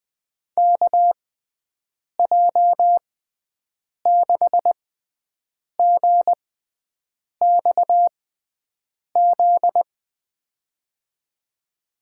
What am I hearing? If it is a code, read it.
Morse code, heard as a keyed tone, about 20 wpm, spaced as for 7 wpm, 700 Hz. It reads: KJ6GXZ
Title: USA Call Signs - 6 Characters - 6x Character Spacing - 20wpm